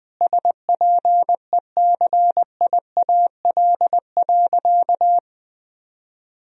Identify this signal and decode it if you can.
Morse code, heard as a keyed tone, about 21 wpm, 700 Hz. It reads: SPECIAL.
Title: Sentences from Top 1000 Words - Review Part 3 - 20wpm